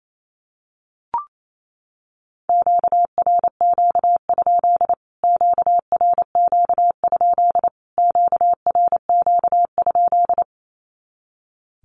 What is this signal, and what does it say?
Morse code, heard as a keyed tone, about 28 wpm, 700 Hz. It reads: QRQ? QRQ? QRQ?
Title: Q-Codes Repeated 3x 28wpm